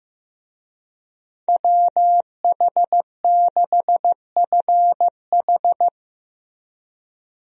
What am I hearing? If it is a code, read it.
Morse code, heard as a keyed tone, about 15 wpm, 700 Hz. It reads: WH6FH
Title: USA Call Signs - 5 Characters - Repeated 3x - 15wpm